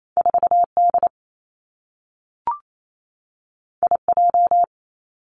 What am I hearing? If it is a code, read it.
Morse code, heard as a keyed tone, about 28 wpm, 700 Hz. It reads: 4 D I J
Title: Two Challenging Letter-Number Combinations 28wpm